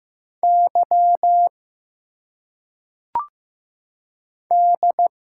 Morse code, heard as a keyed tone, about 15 wpm, 700 Hz. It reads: Y D
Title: Single Letters - Phonetic Pronunciation 15wpm